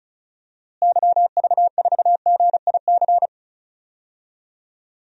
Morse code, heard as a keyed tone, about 36 wpm, 700 Hz. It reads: YV4GIC
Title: DXCC-100 Call Signs - 6 Characters - Repeated 3x - 35wpm